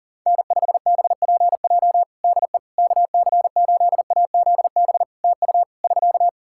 Morse code, heard as a keyed tone, about 40 wpm, 700 Hz. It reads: N5BPJ DE KC8AZB TU <SK>
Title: K1USN SST QSO Practice 40wpm